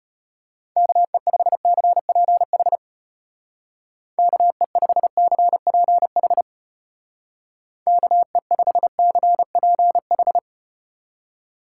Morse code, heard as a keyed tone, about 33 wpm, 700 Hz. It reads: KE5CPH KE5CPH KE5CPH